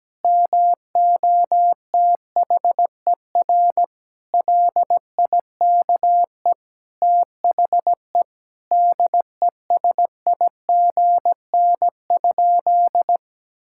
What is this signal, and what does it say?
Morse code, heard as a keyed tone, about 17 wpm, 700 Hz. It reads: MOTHER LIKE THE DESIGN?
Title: Sentences from Top 1000 Words - Review Part 5 - 17wpm